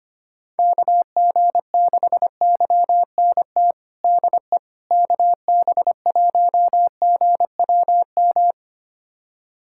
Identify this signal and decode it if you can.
Morse code, heard as a keyed tone, about 25 wpm, 700 Hz. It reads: KG6YNT DE KB1GWM
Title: USA Callsign DE Callsign - 6 Characters - 25wpm